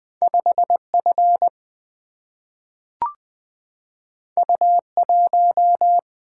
Morse code, heard as a keyed tone, about 20 wpm, 700 Hz. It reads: 5 F U 1
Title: Single-Letter Single-Number 20wpm